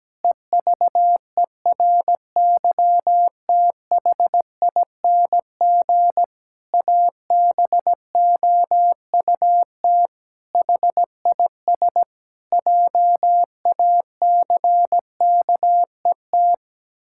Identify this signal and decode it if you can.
Morse code, heard as a keyed tone, about 17 wpm, 700 Hz. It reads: EVERYTHING ABOUT HIS JACKET
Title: Pangram Challenge v2 - 17wpm